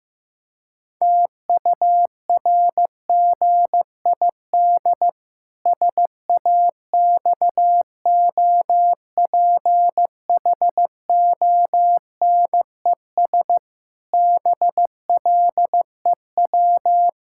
Morse code, heard as a keyed tone, about 16 wpm, 700 Hz. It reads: TURGID SAXOPHONES BLEW
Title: Pangram Challenge v2 - 15wpm